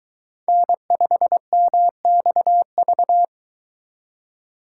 Morse code, heard as a keyed tone, about 22 wpm, 700 Hz. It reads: N5MXV